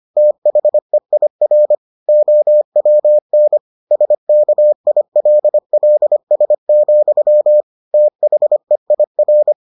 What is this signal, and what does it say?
Morse code, heard as a keyed tone, about 25 wpm, 600 Hz. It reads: THEIR OWN SKILLS, THEIR